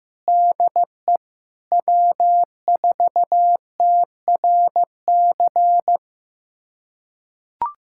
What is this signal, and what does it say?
Morse code, heard as a keyed tone, about 15 wpm, 700 Hz. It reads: DE W4TRC
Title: USA Callsign DE Callsign - 5 Characters - 15wpm